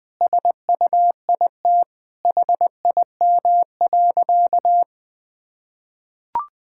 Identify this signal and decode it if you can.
Morse code, heard as a keyed tone, about 20 wpm, 700 Hz. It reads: SUIT HIM.
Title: Sentences from Top 1600 Words 20wpm